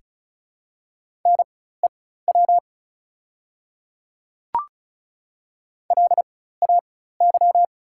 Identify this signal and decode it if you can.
Morse code, heard as a keyed tone, about 36 wpm, spaced as for 16 wpm, 690 Hz. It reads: NEW LAY
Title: 4x Character Spacing - Top 500 Words - 35wpm